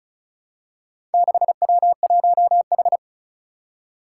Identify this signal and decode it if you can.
Morse code, heard as a keyed tone, about 36 wpm, 700 Hz. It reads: 6W1H
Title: Four Challenging Letter-Number Combinations 35wpm